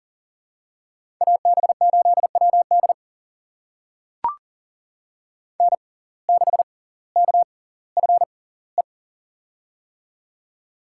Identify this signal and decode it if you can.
Morse code, heard as a keyed tone, about 40 wpm, spaced as for 20 wpm, 700 Hz. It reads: AB8WD N6KFE
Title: USA Call Signs - 6x Character Spacing - 40wpm